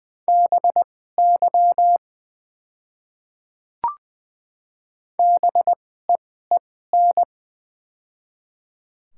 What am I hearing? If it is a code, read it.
Morse code, heard as a keyed tone, about 21 wpm, spaced as for 15 wpm, 700 Hz. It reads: BY BEEN
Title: 2x Character Spacing - Top 500 Words - 20wpm